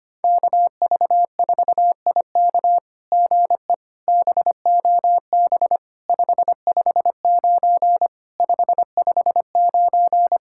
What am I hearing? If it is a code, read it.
Morse code, heard as a keyed tone, about 24 wpm, 700 Hz. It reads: KV4IK GE BOB 559 559
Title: SOTA QSO Practice 25wpm